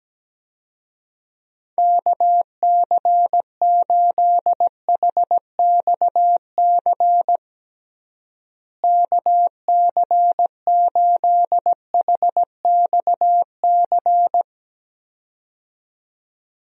Morse code, heard as a keyed tone, about 17 wpm, 700 Hz. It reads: KC8HXC KC8HXC
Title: USA Call Signs - 6 Characters - Repeated 2x - 17wpm